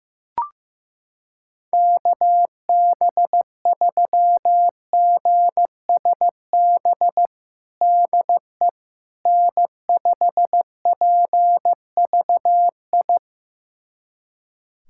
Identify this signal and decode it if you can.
Morse code, heard as a keyed tone, about 15 wpm, 700 Hz. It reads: KB3GSB DE N5PVI
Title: USA Callsign DE Callsign 15wpm